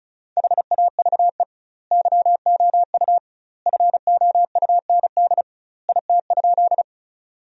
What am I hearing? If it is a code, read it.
Morse code, heard as a keyed tone, about 36 wpm, 700 Hz. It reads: HAVE YOU FOUND IT?
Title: 2x Word-Spacing - Sentences from Top 300 Words - 35wpm